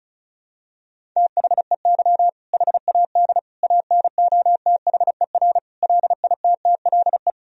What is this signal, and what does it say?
Morse code, heard as a keyed tone, about 34 wpm, 700 Hz. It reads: THEY HAD ANOTHER LITTLE
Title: Sentences from Top 1000 Words - Review Part 1 - 35wpm